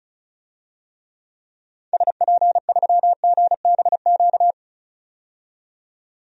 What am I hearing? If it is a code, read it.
Morse code, heard as a keyed tone, about 36 wpm, 700 Hz. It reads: SP3GBQ
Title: DXCC-100 Call Signs - 6 Characters - 35wpm